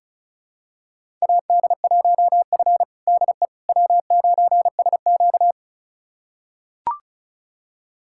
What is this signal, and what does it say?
Morse code, heard as a keyed tone, about 36 wpm, 700 Hz. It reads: AD1F DE W9SQ
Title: USA Callsign DE Callsign - 4 Characters - 35wpm